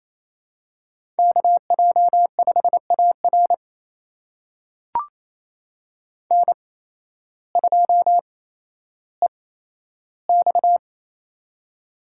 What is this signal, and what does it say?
Morse code, heard as a keyed tone, about 28 wpm, spaced as for 13 wpm, 700 Hz. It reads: KJ5AR N2EX
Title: USA Call Signs - 8x Character Spacing - 28wpm